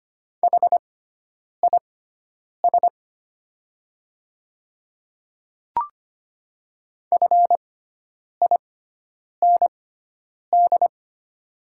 Morse code, heard as a keyed tone, about 25 wpm, spaced as for 9 wpm, 700 Hz. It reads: HIS FIND
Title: 6x Character Spacing - Top 100 Words - 25wpm